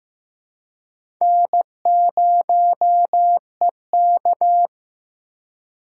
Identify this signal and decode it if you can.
Morse code, heard as a keyed tone, about 15 wpm, 700 Hz. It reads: N0EK
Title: USA Call Signs - 4 Characters - 3x Character Spacing - 15wpm